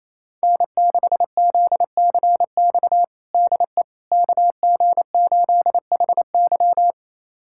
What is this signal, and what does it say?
Morse code, heard as a keyed tone, about 28 wpm, 700 Hz. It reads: N6ZCX DE KG8HY
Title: USA Callsign DE Callsign 28wpm